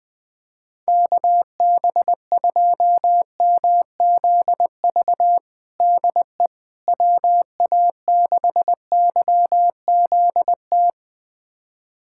Morse code, heard as a keyed tone, about 20 wpm, 700 Hz. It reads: KB2MZV DE WA6YZT
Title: USA Callsign DE Callsign - 6 Characters - 20wpm